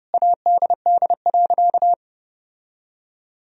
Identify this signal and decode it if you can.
Morse code, heard as a keyed tone, about 31 wpm, 700 Hz. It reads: ADD.
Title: Sentences from Top 1000 Words - Review Part 1 - 30wpm